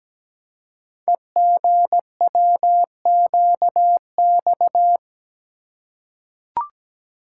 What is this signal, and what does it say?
Morse code, heard as a keyed tone, about 17 wpm, 700 Hz. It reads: EGWQX
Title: Code Groups - 5 Letters - 17wpm